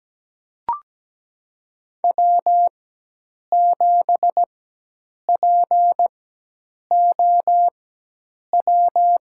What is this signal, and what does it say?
Morse code, heard as a keyed tone, about 17 wpm, spaced as for 8 wpm, 700 Hz. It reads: W7POW
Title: USA Call Signs - 4x Character Spacing - 17wpm